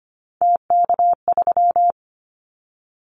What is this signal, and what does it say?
Morse code, heard as a keyed tone, about 25 wpm, 700 Hz. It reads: TK3
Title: Callsign Prefixes for Top 100 DXCC - 25wpm